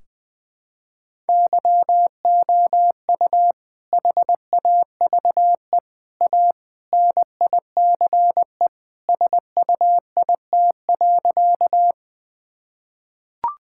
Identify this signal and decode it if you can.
Morse code, heard as a keyed tone, about 20 wpm, 700 Hz. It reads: YOU HAVE A NICE SUIT.
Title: Sentences from Top 1600 Words 20wpm